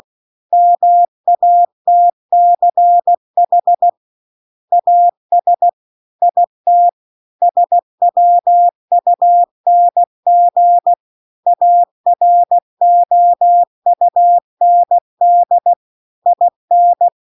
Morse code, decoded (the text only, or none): MATCH AS IT SWUNG AROUND IN